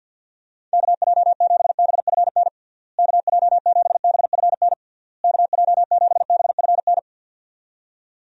Morse code, heard as a keyed tone, about 49 wpm, 700 Hz. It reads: KJ7BFN KJ7BFN KJ7BFN
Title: USA Call Signs - 6 Characters - Repeated 3x - 50wpm